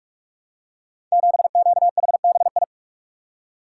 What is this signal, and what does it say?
Morse code, heard as a keyed tone, about 44 wpm, 690 Hz. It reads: 7QHBI
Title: Code Groups - 5 Letter:Numbers - 45wpm